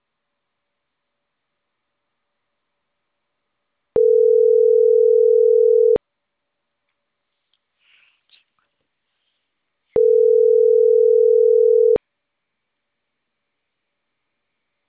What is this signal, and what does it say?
Ringback tone (North America), 440 + 480 Hz.